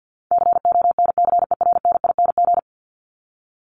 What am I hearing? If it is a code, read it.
Morse code, heard as a keyed tone, about 48 wpm, 700 Hz. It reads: CONCERNING
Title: Top 1751-2000 Words 50wpm